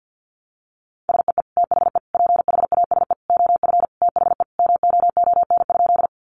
Morse code, heard as a keyed tone, about 50 wpm, 700 Hz. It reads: SEE THE PHASE OF THE MOON?
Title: Sentences from Top 1200 Words 50wpm